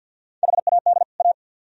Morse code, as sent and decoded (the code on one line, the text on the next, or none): .... .- -.. / .-
HAD A